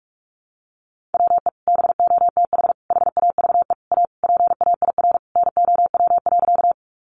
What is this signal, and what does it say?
Morse code, heard as a keyed tone, about 44 wpm, 700 Hz. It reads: WE BOTH HAVE A PAIR NOW.
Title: Sentences from Top 1200 Words 45wpm